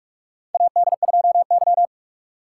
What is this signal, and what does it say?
Morse code, heard as a keyed tone, about 43 wpm, 700 Hz. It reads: AD2Y
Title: USA Call Signs - 2x Character Spacing - 45wpm